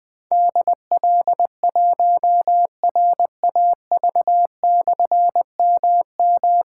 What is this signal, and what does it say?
Morse code, heard as a keyed tone, about 20 wpm, 700 Hz. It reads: DL1RAV/MM
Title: Difficult Call Signs 20wpm